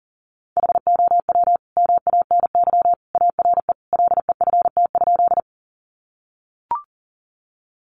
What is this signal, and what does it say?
Morse code, heard as a keyed tone, about 40 wpm, 700 Hz. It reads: HOW MANY ARE LEFT?
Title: Sentences from Top 300 Words 40wpm